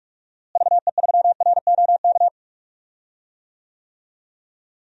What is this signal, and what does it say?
Morse code, heard as a keyed tone, about 46 wpm, 700 Hz. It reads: VE3ROK
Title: DXCC-100 Call Signs - 6 Characters - Repeated 2x - 45wpm